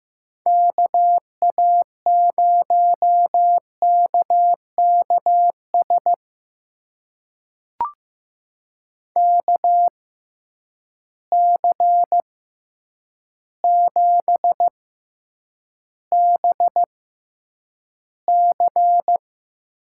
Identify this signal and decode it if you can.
Morse code, heard as a keyed tone, about 15 wpm, spaced as for 8 wpm, 700 Hz. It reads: KA0KKS KC7BC
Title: USA Call Signs - 6x Character Spacing - 15wpm